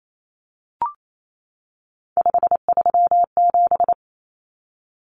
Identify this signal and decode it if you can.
Morse code, heard as a keyed tone, about 28 wpm, 700 Hz. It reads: 537
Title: Numbers - 3-Digit 28wpm